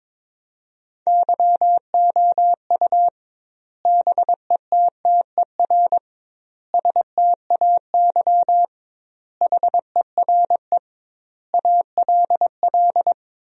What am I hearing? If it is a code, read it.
Morse code, heard as a keyed tone, about 22 wpm, 700 Hz. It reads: YOU BETTER STAY HERE ALL
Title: 2x Word-Spacing - Sentences from Top 500 Words - 22wpm